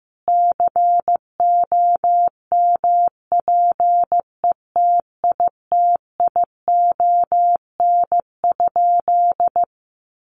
Morse code, heard as a keyed tone, about 15 wpm, 700 Hz. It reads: COMPETITION?
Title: Sentences from Top 1000 Words 15wpm